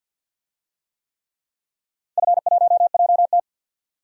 Morse code, heard as a keyed tone, about 50 wpm, 700 Hz. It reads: F1JT
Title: DXCC-100 Call Signs - 4 Characters - 50wpm